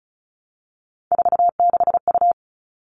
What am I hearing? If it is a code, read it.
Morse code, heard as a keyed tone, about 34 wpm, 700 Hz. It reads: 46U
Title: All Cut Numbers - 3-Digit 35wpm